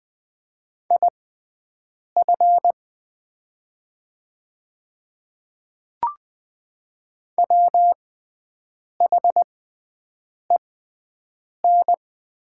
Morse code, heard as a keyed tone, about 20 wpm, spaced as for 7 wpm, 690 Hz. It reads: IF WHEN
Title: 6x Character Spacing - Top 100 Words - 20wpm